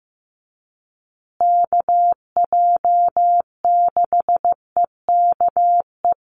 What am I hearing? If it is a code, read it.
Morse code, heard as a keyed tone, about 15 wpm, 700 Hz. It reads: KJ6EKE
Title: USA Call Signs Repeated 2x 15wpm